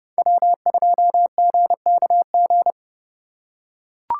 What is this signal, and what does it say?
Morse code, heard as a keyed tone, about 28 wpm, 700 Hz. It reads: W2GKG